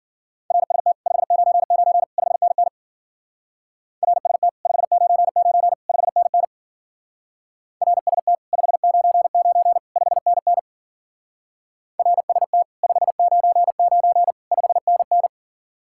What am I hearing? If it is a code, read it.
Morse code, heard as a keyed tone, about 50 wpm, 700 Hz. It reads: RST 599 5NN RST 599 5NN RST 599 5NN RST 599 5NN